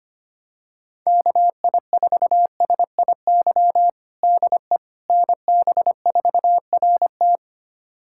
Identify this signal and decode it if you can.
Morse code, heard as a keyed tone, about 25 wpm, 700 Hz. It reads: KI4SIY DE NB4RT
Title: USA Callsign DE Callsign 25wpm